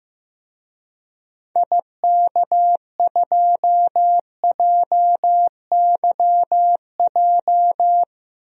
Morse code, heard as a keyed tone, about 15 wpm, 700 Hz. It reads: IK2JYJ
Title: DXCC-100 Call Signs - 6 Characters - 15wpm